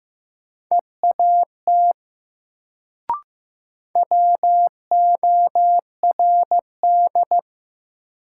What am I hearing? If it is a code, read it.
Morse code, heard as a keyed tone, about 15 wpm, 700 Hz. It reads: EAT WORD
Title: Sentences from Top 300 Words - New Word Drill - 15wpm